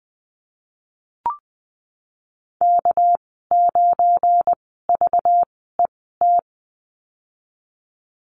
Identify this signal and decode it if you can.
Morse code, heard as a keyed tone, about 20 wpm, spaced as for 15 wpm, 700 Hz. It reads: K9VET
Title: 213 Lesson 34g - Number 8 - Call Signs 20wpm